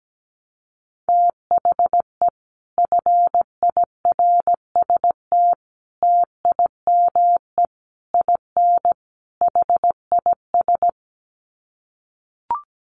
Morse code, heard as a keyed tone, about 17 wpm, 700 Hz. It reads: THE FIRST TIME IN HIS
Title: Sets of 5 Words Encore2 17wpm